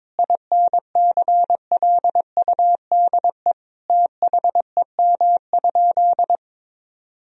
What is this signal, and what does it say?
Morse code, heard as a keyed tone, about 22 wpm, 700 Hz. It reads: INCLUDE THEM?
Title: Sentences from Top 1000 Words - Review Part 3 - 22wpm